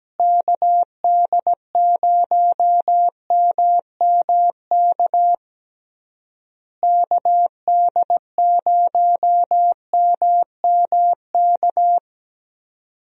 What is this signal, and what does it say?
Morse code, heard as a keyed tone, about 17 wpm, 700 Hz. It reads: KD0MMK KD0MMK
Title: USA Call Signs - 6 Characters - Repeated 3x - 17wpm